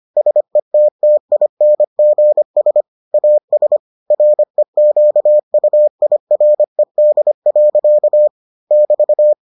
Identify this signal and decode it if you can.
Morse code, heard as a keyed tone, about 25 wpm, 600 Hz. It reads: SETTINGS AS REQUIRED. <BT>